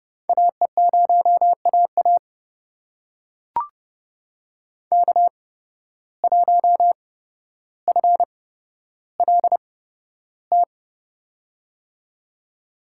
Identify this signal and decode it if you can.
Morse code, heard as a keyed tone, about 30 wpm, spaced as for 13 wpm, 700 Hz. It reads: AE0AA K1FLT
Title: USA Call Signs - 5 Characters - 8x Character Spacing - 30wpm